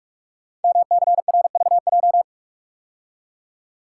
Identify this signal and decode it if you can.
Morse code, heard as a keyed tone, about 44 wpm, 700 Hz. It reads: MCRVJ
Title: Code Groups - 5 Letters - 45wpm